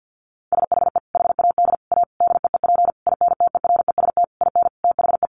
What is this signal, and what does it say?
Morse code, heard as a keyed tone, about 50 wpm, 700 Hz. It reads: SHE HAD A DEEP INTEREST IN THE
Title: Sentences from Top 1300 Words 50wpm